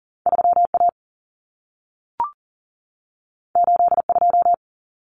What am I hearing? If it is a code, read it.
Morse code, heard as a keyed tone, about 40 wpm, 700 Hz. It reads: 3A 82
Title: Common Cut Numbers - 2-Digit 40wpm